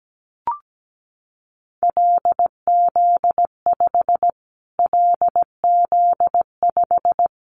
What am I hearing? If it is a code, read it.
Morse code, heard as a keyed tone, about 17 wpm, 700 Hz. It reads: LZ5 LZ5
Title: Callsign Prefixes for Top 25 DXCC - Repeated 3x - 17wpm